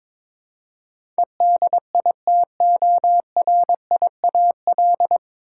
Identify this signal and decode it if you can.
Morse code, heard as a keyed tone, about 22 wpm, 700 Hz. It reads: EDITORIAL
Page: Morse Code Ninja practice 22wpm